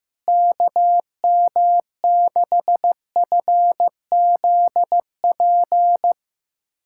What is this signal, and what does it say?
Morse code, heard as a keyed tone, about 15 wpm, 700 Hz. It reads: KM6FZP
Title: USA Callsign DE Callsign - 6 Characters - 15wpm